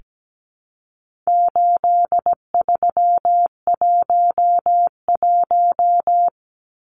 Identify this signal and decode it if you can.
Morse code, heard as a keyed tone, about 17 wpm, 700 Hz. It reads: 8311
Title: Numbers - 4-Digit 17wpm